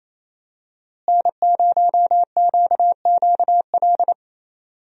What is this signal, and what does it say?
Morse code, heard as a keyed tone, about 28 wpm, 700 Hz. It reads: N0QQL
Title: USA Call Signs - 5 Characters - Repeated 3x - 28wpm